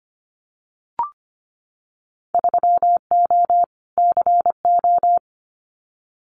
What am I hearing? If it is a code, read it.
Morse code, heard as a keyed tone, about 25 wpm, 700 Hz. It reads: 3O CO
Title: Winter Field Day Exchanges Encore 25wpm